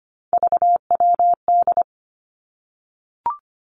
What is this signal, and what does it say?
Morse code, heard as a keyed tone, about 25 wpm, 700 Hz. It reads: VWD
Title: Three Challenging Letter-Number Combinations 25wpm